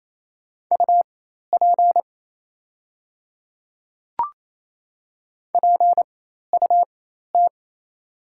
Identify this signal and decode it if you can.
Morse code, heard as a keyed tone, about 28 wpm, spaced as for 13 wpm, 700 Hz. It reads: UP PUT
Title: 4x Character Spacing - Top 300 Words - 28wpm